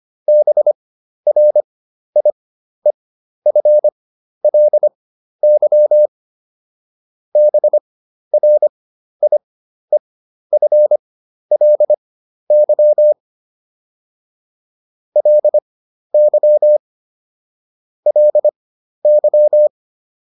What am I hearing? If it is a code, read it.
Morse code, heard as a keyed tone, about 25 wpm, spaced as for 12 wpm, 600 Hz. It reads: BRIEFLY BRIEFLY LY LY